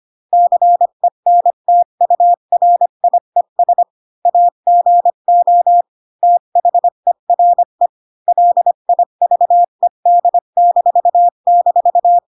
Morse code, heard as a keyed tone, about 25 wpm, 700 Hz. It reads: CENTURIES AGO THERE LIVED--
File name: Pinocchio / Pinocchio ch1 20 25 ik1hge